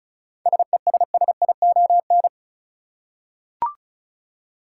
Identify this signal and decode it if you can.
Morse code, heard as a keyed tone, about 36 wpm, 700 Hz. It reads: SESSION
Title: Seven Letter Words - 35wpm